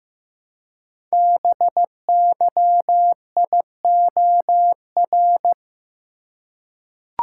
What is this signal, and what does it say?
Morse code, heard as a keyed tone, about 15 wpm, 700 Hz. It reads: BYIOR
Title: Code Groups - 5 Letters - 15wpm